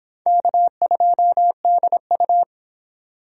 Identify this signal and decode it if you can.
Morse code, heard as a keyed tone, about 22 wpm, 700 Hz. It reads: K2DU